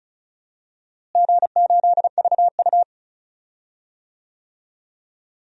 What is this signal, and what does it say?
Morse code, heard as a keyed tone, about 36 wpm, 700 Hz. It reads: G8VU
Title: DXCC-100 Call Signs - 4 Characters - Repeated 2x - 35wpm